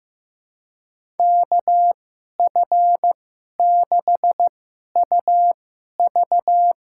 Morse code, heard as a keyed tone, about 15 wpm, spaced as for 11 wpm, 700 Hz. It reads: KF6UV
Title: USA Call Signs - 2x Character Spacing - 15wpm